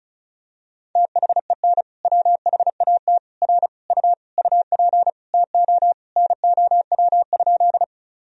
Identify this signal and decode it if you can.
Morse code, heard as a keyed tone, about 34 wpm, 700 Hz. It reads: THEN WHAT R U UP TO NOW?
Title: Sentences from Top 100 Words and Abbreviations 35wpm